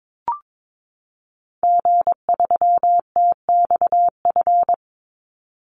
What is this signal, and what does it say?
Morse code, heard as a keyed tone, about 22 wpm, 700 Hz. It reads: G3TXF
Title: Call Signs Repeated 3x - 22wpm